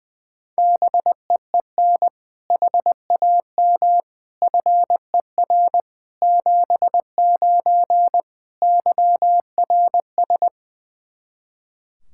Morse code, heard as a keyed tone, about 20 wpm, 700 Hz. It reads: BEEN HAM FER 79 YRS
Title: Practice QSO - Exchange 3 - More Info - 20wpm